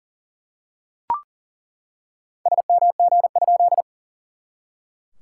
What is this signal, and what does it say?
Morse code, heard as a keyed tone, about 38 wpm, 690 Hz. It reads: SMG?
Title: SOTA QSO Practice 40wpm